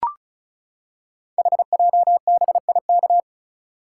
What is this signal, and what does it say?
Morse code, heard as a keyed tone, about 36 wpm, 690 Hz. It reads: HJBIK
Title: Code Groups - 5 Letters - 35wpm